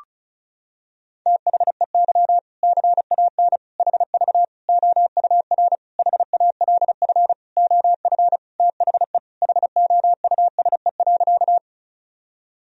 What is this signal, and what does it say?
Morse code, heard as a keyed tone, about 36 wpm, 700 Hz. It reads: THEY CAN HV OUR HALF OF THE HOUSE.
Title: Sentences from Top 400 Words and Abbreviations 35wpm